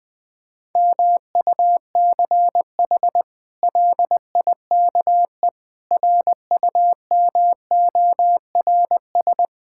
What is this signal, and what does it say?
Morse code, heard as a keyed tone, about 20 wpm, 700 Hz. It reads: MUCH LIKE RUMORS
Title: Sets of 8 Words - 20wpm